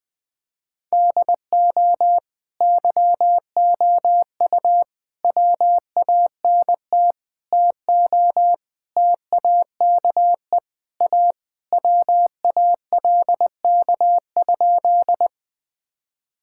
Morse code, heard as a keyed tone, about 20 wpm, 700 Hz. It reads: DO YOU WANT TO TAKE A WALK?
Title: Sentences with 3-Word Phrases - 20wpm